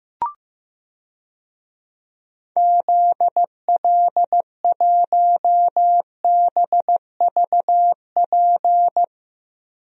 Morse code, heard as a keyed tone, about 15 wpm, 700 Hz. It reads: ZL1BVP
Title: DXCC-100 Call Signs - 6 Characters - Repeated 2x - 15wpm